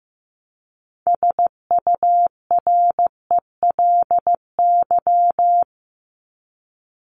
Morse code, heard as a keyed tone, about 15 wpm, 700 Hz. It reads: SURELY
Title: LY Suffix - Top 100 Words with Base Words 15wpm